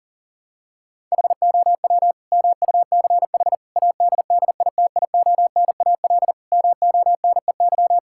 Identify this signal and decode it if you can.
Morse code, heard as a keyed tone, about 40 wpm, 700 Hz. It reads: HOW MUCH ADDITIONAL MONEY
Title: Sentences from Top 1000 Words - Review Part 5 - 40wpm